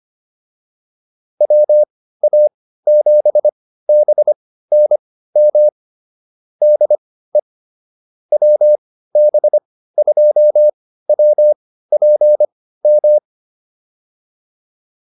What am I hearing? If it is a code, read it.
Morse code, heard as a keyed tone, about 25 wpm, spaced as for 15 wpm, 600 Hz. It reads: WA7BNM DE WB2WPM
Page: intermediate-practice-files